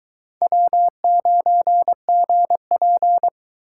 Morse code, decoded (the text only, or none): W9GP